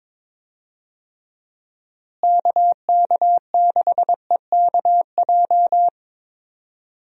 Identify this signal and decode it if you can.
Morse code, heard as a keyed tone, about 22 wpm, 700 Hz. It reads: KK6EKJ
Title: USA Call Signs - 6 Characters - Repeated 3x - 22wpm